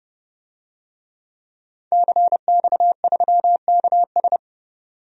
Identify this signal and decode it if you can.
Morse code, heard as a keyed tone, about 30 wpm, 700 Hz. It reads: CX3KS
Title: DXCC-100 Call Signs - 5 Characters - 30wpm